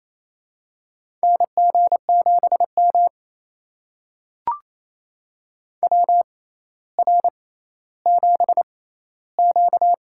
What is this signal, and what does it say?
Morse code, heard as a keyed tone, about 28 wpm, spaced as for 14 wpm, 700 Hz. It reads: NG7M WR7Q
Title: USA Call Signs - 4 Characters - 6x Character Spacing - 28wpm